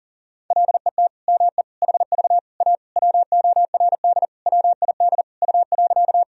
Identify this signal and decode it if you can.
Morse code, heard as a keyed tone, about 40 wpm, 700 Hz. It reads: LET ME HV A WORD WID U.
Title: Sentences from Top 300 Words and Abbreviations 40wpm